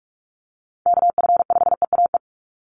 Morse code, heard as a keyed tone, about 44 wpm, 700 Hz. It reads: KF5EAE
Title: USA Call Signs 45wpm